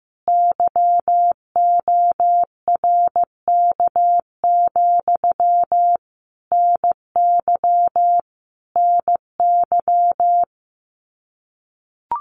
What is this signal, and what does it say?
Morse code, heard as a keyed tone, about 15 wpm, 700 Hz. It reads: YORK, NY NY
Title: Common QSO Phrases Encore 15wpm